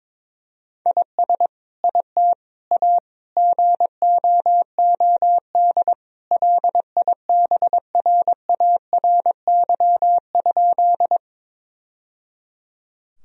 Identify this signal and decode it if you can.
Morse code, heard as a keyed tone, about 21 wpm, 700 Hz. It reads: IS IT A GOOD LIBRARY?
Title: Sentences from Top 1000 Words - Review Part 4 - 22wpm